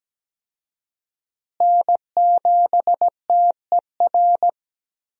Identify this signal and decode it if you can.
Morse code, heard as a keyed tone, about 17 wpm, 700 Hz. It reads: N7TER
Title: USA Call Signs - 5 Characters - Repeated 3x - 17wpm